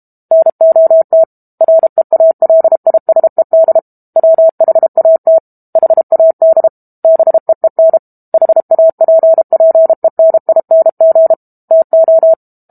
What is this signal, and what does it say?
Morse code, heard as a keyed tone, about 32 wpm, 650 Hz. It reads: NOT REALISED WHAT HAD BEEN HAPPENING TO
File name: Book Two / 06-Chapter 6 - 32 WPM